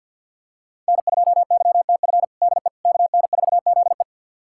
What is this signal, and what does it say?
Morse code, heard as a keyed tone, about 50 wpm, 690 Hz. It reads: N2YTF DE KN4ZE
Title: USA Callsign DE Callsign - 5 Characters - 50wpm